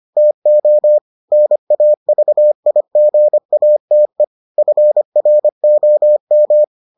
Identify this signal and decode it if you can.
Morse code, heard as a keyed tone, about 25 wpm, 600 Hz. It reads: TO NAVIGATE FROM